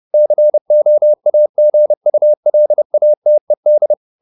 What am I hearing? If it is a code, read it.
Morse code, heard as a keyed tone, about 30 wpm, 600 Hz. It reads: COAGULATED
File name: the old man and the sea / the old man and the sea 1 30wpm